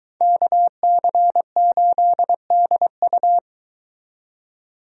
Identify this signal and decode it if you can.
Morse code, heard as a keyed tone, about 23 wpm, 700 Hz. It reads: KC8DU